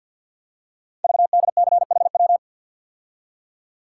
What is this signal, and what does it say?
Morse code, heard as a keyed tone, about 48 wpm, 700 Hz. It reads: VDCHW
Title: Code Groups - 5 Letters - 50wpm